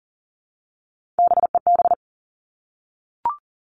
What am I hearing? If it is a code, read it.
Morse code, heard as a keyed tone, about 40 wpm, 700 Hz. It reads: BEB
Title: All Cut Numbers - 3-Digit 40wpm